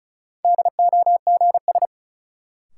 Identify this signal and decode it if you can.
Morse code, heard as a keyed tone, about 36 wpm, 700 Hz. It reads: DOGS
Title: Sentences from Top 1600 Words - New Word Drill - 35wpm